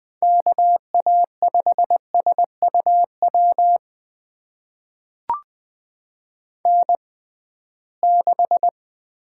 Morse code, heard as a keyed tone, about 20 wpm, spaced as for 15 wpm, 700 Hz. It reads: KA5SUW N6
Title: USA Call Signs - 6x Character Spacing - 20wpm